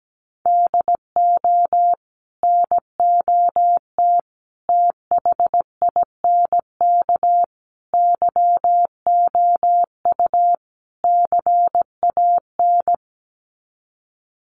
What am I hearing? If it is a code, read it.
Morse code, heard as a keyed tone, about 17 wpm, 700 Hz. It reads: DO NOT THINK YOU CAN
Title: Sets of 5 Words 17wpm